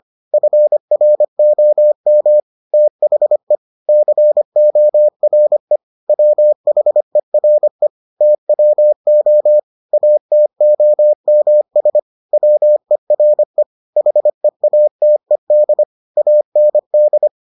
Morse code, heard as a keyed tone, about 25 wpm, 600 Hz. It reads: FROM THE CORE WHERE TWO ATOMS WERE HEATED AND